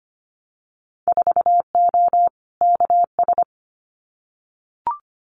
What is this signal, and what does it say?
Morse code, heard as a keyed tone, about 25 wpm, 700 Hz. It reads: 4O KS
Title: Winter Field Day Exchanges Encore 25wpm